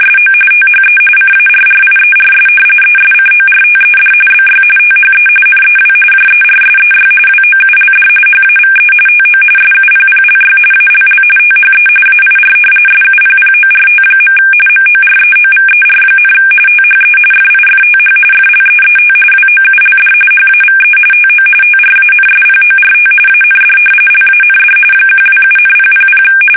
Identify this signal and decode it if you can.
RTTY, about 45.45 baud. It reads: SO THE CARRIERS ESCORTED BY THE NEW JERSEY WOULD HAVE TO SUPPLY ALL THE AIR SUPPORT. MOREOVER, THE FIFTH FLEET WOULD HAVE TO TRANSPORT AND LAND 127,000
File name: fsk850 / battleshipnj